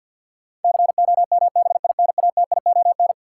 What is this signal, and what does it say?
Morse code, heard as a keyed tone, about 51 wpm, 700 Hz. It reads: COMBINATION
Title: Sentences from Top 1300 Words - New Word Drill Alt - 50wpm